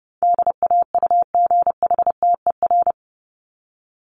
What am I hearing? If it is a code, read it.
Morse code, heard as a keyed tone, about 30 wpm, 700 Hz. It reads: DAUGHTER
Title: Top 1251-1500 Words 30wpm